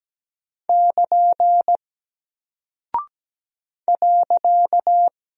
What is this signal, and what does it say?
Morse code, heard as a keyed tone, about 17 wpm, 700 Hz. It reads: <KN> .
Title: Single Character (Letter:Number:Punctuation:Prosign) Alt 17wpm